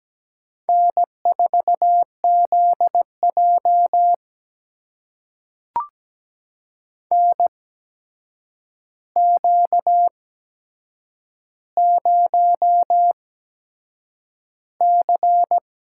Morse code, heard as a keyed tone, about 17 wpm, spaced as for 7 wpm, 700 Hz. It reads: N4ZJ NQ0C
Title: USA Call Signs - 4 Characters - 8x Character Spacing - 17wpm